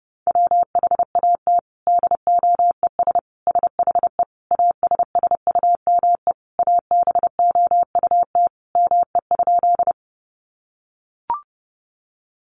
Morse code, heard as a keyed tone, about 30 wpm, 700 Hz. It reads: WHAT DOES SHE ASSUME ABOUT ME?
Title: Sentences from Top 1500 Words 30wpm